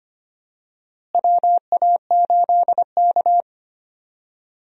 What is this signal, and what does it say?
Morse code, heard as a keyed tone, about 25 wpm, 700 Hz. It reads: WA8K
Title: USA Call Signs - 4 Characters - Repeated 2x - 25wpm